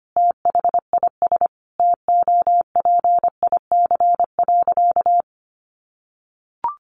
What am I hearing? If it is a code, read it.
Morse code, heard as a keyed tone, about 25 wpm, 700 Hz. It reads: THIS TOPIC.
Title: Sentences from Top 1200 Words 25wpm